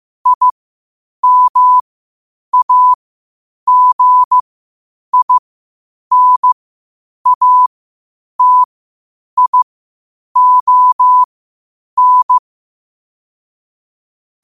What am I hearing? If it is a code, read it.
Morse code, heard as a keyed tone, about 15 wpm, spaced as for 8 wpm, 1000 Hz. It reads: IMAGINATION
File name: qotdfiles / 8wpm